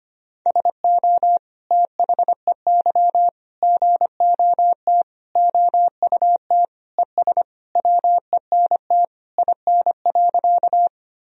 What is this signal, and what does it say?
Morse code, heard as a keyed tone, about 25 wpm, 700 Hz. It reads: SO THEY GOT OUT ES WENT IN.
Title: Sentences from Top 300 Words and Abbreviations 25wpm